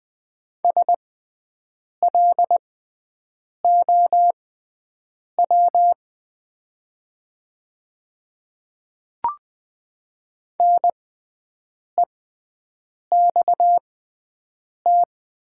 Morse code, heard as a keyed tone, about 20 wpm, spaced as for 7 wpm, 700 Hz. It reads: SLOW NEXT
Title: 6x Character Spacing - Top 500 Words - 20wpm